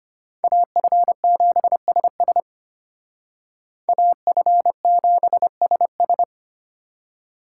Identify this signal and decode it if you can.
Morse code, heard as a keyed tone, about 32 wpm, 700 Hz. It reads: AF7SS AF7SS